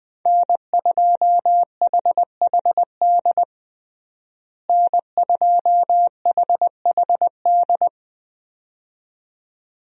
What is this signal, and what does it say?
Morse code, heard as a keyed tone, about 21 wpm, 700 Hz. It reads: N2HHD N2HHD
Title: USA Call Signs - 5 Characters - Repeated 2x - 20wpm